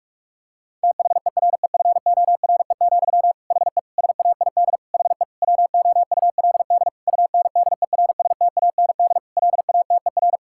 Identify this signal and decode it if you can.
Morse code, heard as a keyed tone, about 46 wpm, 700 Hz. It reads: THEREFORE, HE SAID HE WOULD UNDERSTAND LATER
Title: Sentences from Top 1000 Words - Review Part 5 - 45wpm